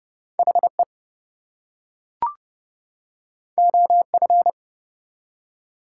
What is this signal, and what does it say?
Morse code, heard as a keyed tone, about 30 wpm, 700 Hz. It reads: HE OF
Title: Two Letter Words - 30wpm